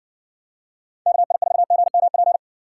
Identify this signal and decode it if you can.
Morse code, heard as a keyed tone, about 60 wpm, 700 Hz. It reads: KI4GMP